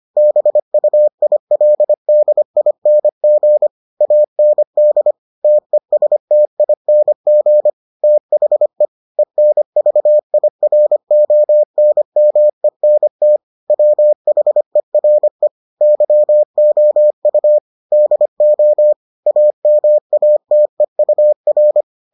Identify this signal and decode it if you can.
Morse code, heard as a keyed tone, about 25 wpm, 600 Hz. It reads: BUILDING AND TESTING THE ENVIRONMENT WHERE YOU DO AMATEUR